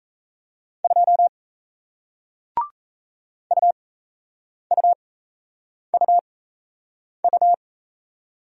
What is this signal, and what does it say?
Morse code, heard as a keyed tone, about 38 wpm, spaced as for 8 wpm, 700 Hz. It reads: 2 UUUU